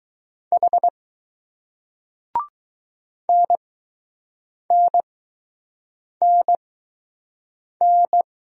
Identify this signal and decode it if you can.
Morse code, heard as a keyed tone, about 20 wpm, spaced as for 6 wpm, 700 Hz. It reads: H NNNN